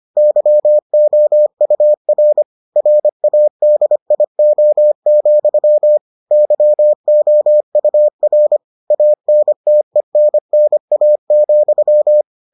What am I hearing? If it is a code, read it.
Morse code, heard as a keyed tone, about 25 wpm, 600 Hz. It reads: YOUR RADIO, YOUR ANTENNA,